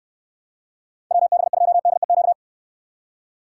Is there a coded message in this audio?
Yes: Morse code from 1.1 to 2.3 s, heard as a keyed tone, about 69 wpm, 700 Hz. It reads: WB2DEQ